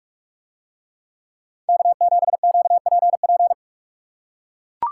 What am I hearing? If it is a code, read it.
Morse code, heard as a keyed tone, about 44 wpm, 700 Hz. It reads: K7QPP